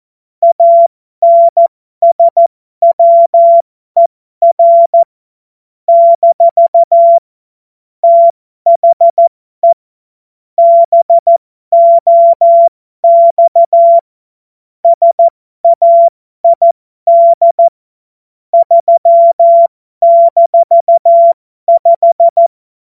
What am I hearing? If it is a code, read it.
Morse code, heard as a keyed tone, about 14 wpm, spaced as for 12 wpm, 680 Hz. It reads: ANSWER - THE BOX SAID 3-5
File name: Morsekurs / Witze